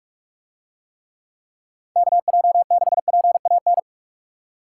Morse code, heard as a keyed tone, about 44 wpm, 700 Hz. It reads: KJ6PAN